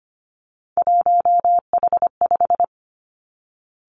Morse code, heard as a keyed tone, about 25 wpm, 700 Hz. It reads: 1H5
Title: Three Challenging Letter-Number Combinations 25wpm